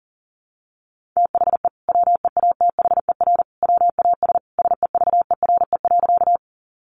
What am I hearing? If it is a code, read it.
Morse code, heard as a keyed tone, about 40 wpm, 700 Hz. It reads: THE WEATHER WAS SEVERE.
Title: Sentences from Top 1400 Words 40wpm